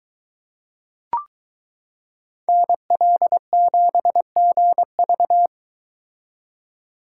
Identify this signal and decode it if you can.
Morse code, heard as a keyed tone, about 22 wpm, 700 Hz. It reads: NL7GV